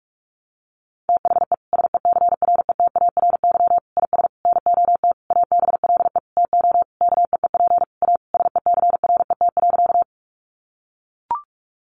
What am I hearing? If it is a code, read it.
Morse code, heard as a keyed tone, about 44 wpm, 700 Hz. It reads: THE SECRETARY IS NOT ABLE TO KEEP A SECRET.
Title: Sentences from Top 1300 Words 45wpm